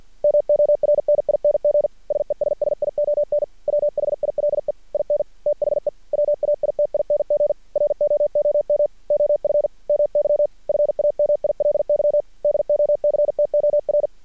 Morse code, heard as a keyed tone, about 50 wpm, 590 Hz. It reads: MORNING SESSION WHILE IN THE WAITING ROOM OF MY FAMILY DOCTOR
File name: archived / CRSnet